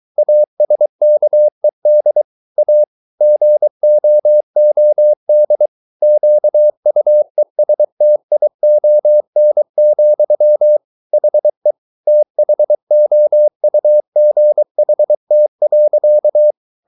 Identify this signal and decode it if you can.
Morse code, heard as a keyed tone, about 23 wpm, 600 Hz. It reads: ASKED A GOOD QUESTION, HE THOUGHT.